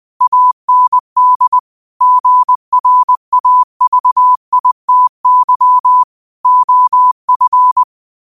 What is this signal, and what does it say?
Morse code, heard as a keyed tone, about 20 wpm, 1000 Hz. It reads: AND GRAVITY OF
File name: qotdfiles / 20wpm